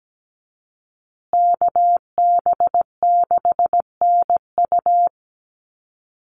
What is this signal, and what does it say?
Morse code, heard as a keyed tone, about 17 wpm, 700 Hz. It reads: KB6NU
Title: Call Signs Repeated 2x - 17wpm